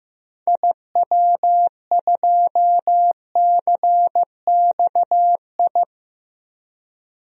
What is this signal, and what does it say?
Morse code, heard as a keyed tone, about 15 wpm, 700 Hz. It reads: IW2CXI
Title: DXCC-100 Call Signs - 6 Characters - 15wpm